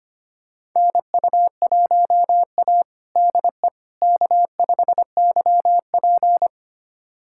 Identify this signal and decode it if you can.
Morse code, heard as a keyed tone, about 25 wpm, 700 Hz. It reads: NU1A DE K5YP
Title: USA Callsign DE Callsign - 4 Characters - 25wpm